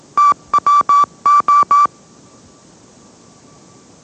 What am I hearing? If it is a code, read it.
Morse code, heard as a keyed tone, about 18 wpm, 1160 Hz. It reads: TWO
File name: mrx / mrx082 20wpm